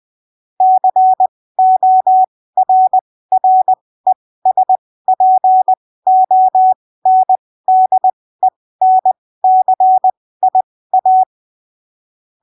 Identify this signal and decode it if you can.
Morse code, heard as a keyed tone, about 20 wpm, spaced as for 15 wpm, 750 Hz. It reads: CORRESPONDENCIA
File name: arquivos / texto / texto 28 20ppm